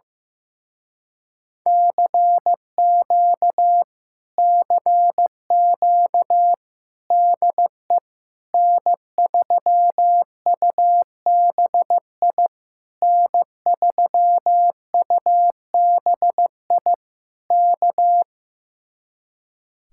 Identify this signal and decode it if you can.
Morse code, heard as a keyed tone, about 15 wpm, 700 Hz. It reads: CQ CQ DE N3UBI N3UBI K
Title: Speed-Racing Common QSO Phrases Encore - 15wpm